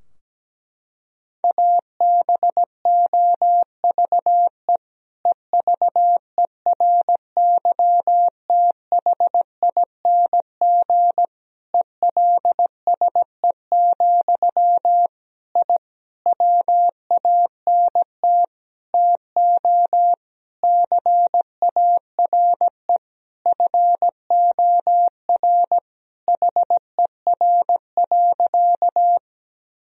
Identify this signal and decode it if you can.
Morse code, heard as a keyed tone, about 17 wpm, 700 Hz. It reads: ABOVE EVERYTHING ELSE, I WANT TO CARE FOR HER.
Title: Sentences from Top 1000 Words - Review Part 2 - 17wpm